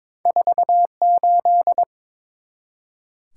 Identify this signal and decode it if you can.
Morse code, heard as a keyed tone, about 22 wpm, 700 Hz. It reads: 48
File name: practice / 22wpm / Two Letter-Number Combinations - 22wpm